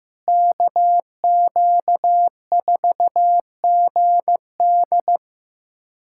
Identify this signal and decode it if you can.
Morse code, heard as a keyed tone, about 15 wpm, 700 Hz. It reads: KQ4GD
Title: USA Callsign DE Callsign - 5 Characters - 15wpm